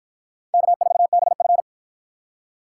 Morse code, heard as a keyed tone, about 52 wpm, 700 Hz. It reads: K4BF